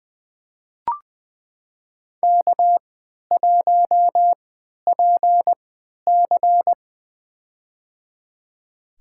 Morse code, heard as a keyed tone, about 20 wpm, spaced as for 11 wpm, 700 Hz. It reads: K1PC
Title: USA Call Signs - 4 Characters - 3x Character Spacing - 20wpm